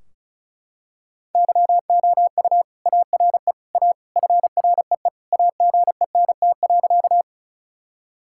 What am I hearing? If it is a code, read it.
Morse code, heard as a keyed tone, about 36 wpm, 700 Hz. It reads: YOU ARE A FREE AGENT.
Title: Sentences from Top 1600 Words 35wpm